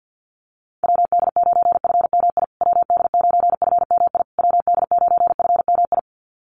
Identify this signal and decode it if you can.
Morse code, heard as a keyed tone, about 50 wpm, 700 Hz. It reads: WD9FMI WD9FMI WD9FMI
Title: USA Call Signs Repeated 3x 50wpm